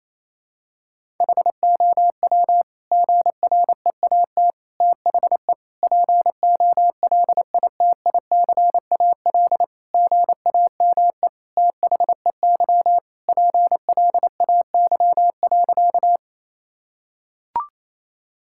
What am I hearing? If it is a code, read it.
Morse code, heard as a keyed tone, about 27 wpm, 700 Hz. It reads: HOW GREAT THE POLITICAL GAME THEY PLAY.
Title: Sentences from Top 1000 Words - Review Part 5 - 28wpm